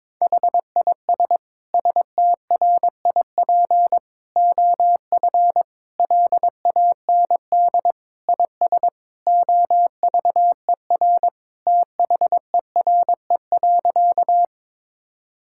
Morse code, heard as a keyed tone, about 22 wpm, 700 Hz. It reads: HIS STRIP OF LAND IS OVER THERE.
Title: Sentences from Top 1600 Words 22wpm